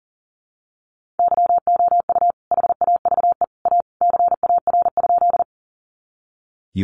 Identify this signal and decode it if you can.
Morse code, heard as a keyed tone, about 40 wpm, 700 Hz. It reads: YOU HAVE A CAR?
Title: Sentences from Top 300 Words 40wpm